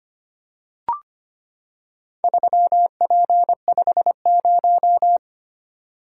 Morse code, heard as a keyed tone, about 25 wpm, 700 Hz. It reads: 3P50
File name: practice / 25wpm / Four Letter-Number Combinations - 25wpm